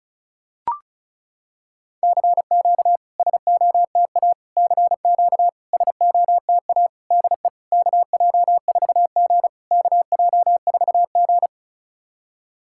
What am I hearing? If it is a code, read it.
Morse code, heard as a keyed tone, about 37 wpm, 700 Hz. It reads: CQ SOTA CQ SOTA DE KJ4G KJ4G
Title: SOTA QSO Practice 35wpm